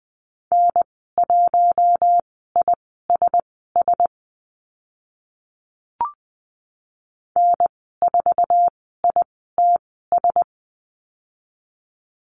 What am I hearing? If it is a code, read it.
Morse code, heard as a keyed tone, about 20 wpm, spaced as for 15 wpm, 700 Hz. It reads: N1ISS N4ITS
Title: 048 Lesson 9g - Number 4 - Call Signs 20wpm